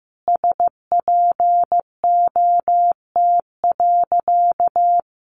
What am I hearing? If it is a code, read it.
Morse code, heard as a keyed tone, about 15 wpm, 700 Hz. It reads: SPOT.
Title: Sentences from Top 1200 Words 15wpm